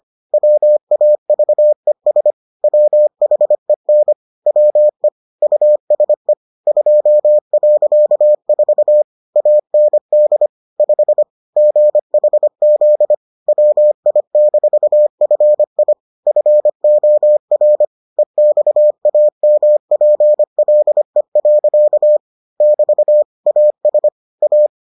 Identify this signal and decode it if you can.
Morse code, heard as a keyed tone, about 25 wpm, 600 Hz. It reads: WAVES WHEN WE USE 2.4 AND 5 GHZ WI-FI FOR EXAMPLE. <BT> AS A